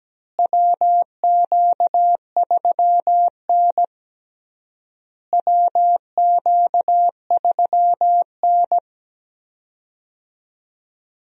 Morse code, heard as a keyed tone, about 17 wpm, 700 Hz. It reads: WQ3N WQ3N
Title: USA Call Signs - 4 Characters - Repeated 3x - 17wpm